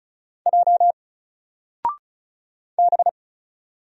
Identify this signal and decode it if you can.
Morse code, heard as a keyed tone, about 34 wpm, 700 Hz. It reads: J B
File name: Single-Letter-Number-35wpm